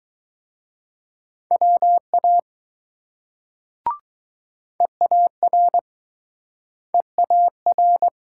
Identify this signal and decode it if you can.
Morse code, heard as a keyed tone, about 20 wpm, 700 Hz. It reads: WA EAR EAR